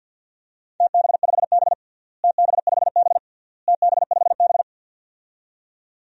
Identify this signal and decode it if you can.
Morse code, heard as a keyed tone, about 50 wpm, 700 Hz. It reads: TB5B TB5B TB5B
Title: DXCC-100 Call Signs - 4 Characters - Repeated 3x - 50wpm